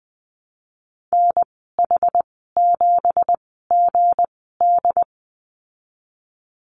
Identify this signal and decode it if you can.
Morse code, heard as a keyed tone, about 20 wpm, spaced as for 15 wpm, 700 Hz. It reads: NH7GD
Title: 179 Lesson 29g - Slash - Call Signs 20wpm